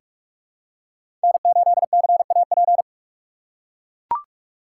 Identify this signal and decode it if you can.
Morse code, heard as a keyed tone, about 41 wpm, 700 Hz. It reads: N8CAP
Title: USA Call Signs - 5 Characters - 3x Character Spacing - 45wpm